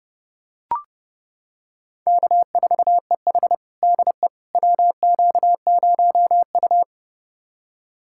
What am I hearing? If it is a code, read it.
Morse code, heard as a keyed tone, about 30 wpm, 700 Hz. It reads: K4EH DE WQ0U
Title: USA Callsign DE Callsign - 4 Characters - 30wpm